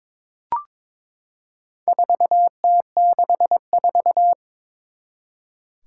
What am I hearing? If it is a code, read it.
Morse code, heard as a keyed tone, about 22 wpm, 700 Hz. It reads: 4T64
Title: Four Challenging Letter-Number Combinations 22wpm